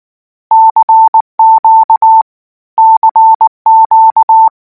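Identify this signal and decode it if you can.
Morse code, heard as a keyed tone, about 19 wpm, 890 Hz. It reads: CQ CQ
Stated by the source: Morse code practice text at 18 wpm, from Portal do CW.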